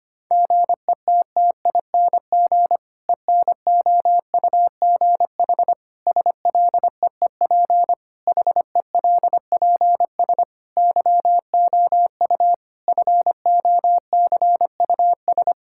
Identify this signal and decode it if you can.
Morse code, heard as a keyed tone, about 25 wpm, 700 Hz. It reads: GETTING ENOUGH SLEEP HELPS YOU FOCUS
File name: practice / 25wpm / Difficult Sentences with Hints - 25wpm